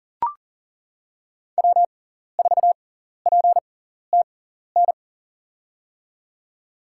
Morse code, heard as a keyed tone, about 40 wpm, spaced as for 14 wpm, 700 Hz. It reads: W4PTN
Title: USA Call Signs - 6x Character Spacing - 40wpm